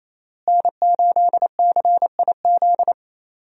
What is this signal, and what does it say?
Morse code, heard as a keyed tone, about 28 wpm, 700 Hz. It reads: N8CIZ
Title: USA Call Signs - 5 Characters - Repeated 2x - 28wpm